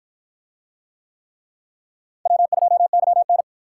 Morse code, heard as a keyed tone, about 53 wpm, 700 Hz. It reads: W2YN